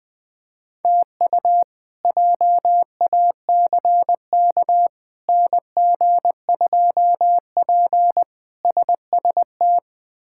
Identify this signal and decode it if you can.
Morse code, heard as a keyed tone, about 20 wpm, 700 Hz. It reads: TU JACK NG2P SST
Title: K1USN SST QSO Practice - 20wpm